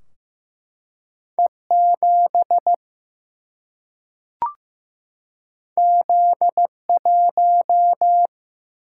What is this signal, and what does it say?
Morse code, heard as a keyed tone, about 15 wpm, 700 Hz. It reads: E7 Z1
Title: Single-Letter Single-Number 15wpm